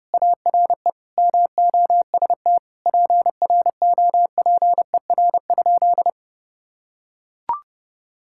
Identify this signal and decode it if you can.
Morse code, heard as a keyed tone, about 30 wpm, 700 Hz. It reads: ARE MOST PROPER?
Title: Sentences from Top 1000 Words - Review Part 5 - 30wpm